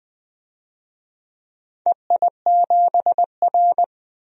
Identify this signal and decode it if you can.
Morse code, heard as a keyed tone, about 20 wpm, 700 Hz. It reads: EI7R
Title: DXCC-100 Call Signs - 4 Characters - 20wpm